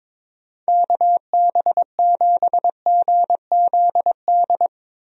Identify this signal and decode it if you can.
Morse code, heard as a keyed tone, about 22 wpm, 700 Hz. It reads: KB7GZD
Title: USA Call Signs - 6 Characters - Repeated 2x - 22wpm